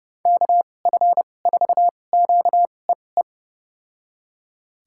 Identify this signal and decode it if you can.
Morse code, heard as a keyed tone, about 30 wpm, spaced as for 22 wpm, 700 Hz. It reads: KF4QEE
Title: USA Call Signs - 6 Characters - 2x Character Spacing - 30wpm